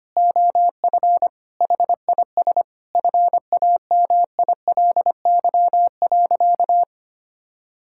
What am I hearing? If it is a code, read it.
Morse code, heard as a keyed tone, about 25 wpm, 700 Hz. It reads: OF HIS FAMILY.
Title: Sentences from Top 1000 Words - Review Part 6 - 25wpm